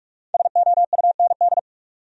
Morse code, heard as a keyed tone, about 60 wpm, spaced as for 52 wpm, 700 Hz. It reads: SOUND
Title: Speed-Racing Top 200 Words 45wpm